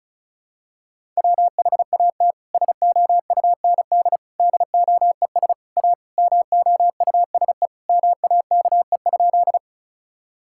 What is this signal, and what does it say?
Morse code, heard as a keyed tone, about 36 wpm, 700 Hz. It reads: WHAT SOUND DOES A MOUSE MAKE?
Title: Sentences from Top 1000 Words - Review Part 5 - 35wpm